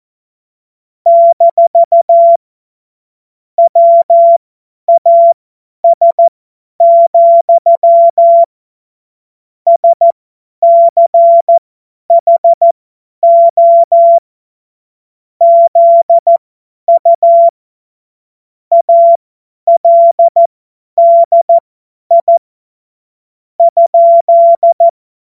Morse code, heard as a keyed tone, about 14 wpm, spaced as for 10 wpm, 680 Hz. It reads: - WAS, SCHO ZU ALDI ?
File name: Witz-30-September-2025-50-F70